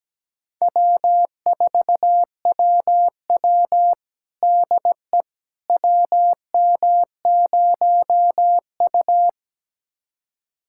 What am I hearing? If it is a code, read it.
Morse code, heard as a keyed tone, about 17 wpm, 700 Hz. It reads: W4WW DE WM0U
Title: USA Callsign DE Callsign - 4 Characters - 17wpm